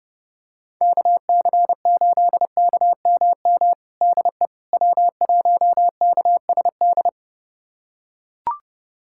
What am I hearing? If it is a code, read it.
Morse code, heard as a keyed tone, about 32 wpm, 700 Hz. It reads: KC8KMM DE W1KSD
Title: USA Callsign DE Callsign 30wpm